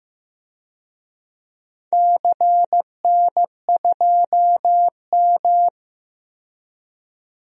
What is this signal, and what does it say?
Morse code, heard as a keyed tone, about 15 wpm, 700 Hz. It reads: CN2M
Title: DXCC-100 Call Signs - 4 Characters - 15wpm